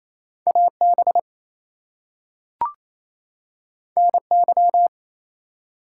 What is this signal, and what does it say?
Morse code, heard as a keyed tone, about 28 wpm, 700 Hz. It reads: AB NY
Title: US States and Canadian Provinces 28wpm